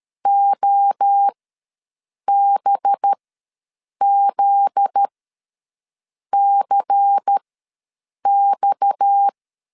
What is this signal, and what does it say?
Morse code, heard as a keyed tone, about 13 wpm, spaced as for 6 wpm, 800 Hz. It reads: OBZCX
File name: LEZIONI / 7 GF9Z / 7 GF9Z ws 12wpm 3321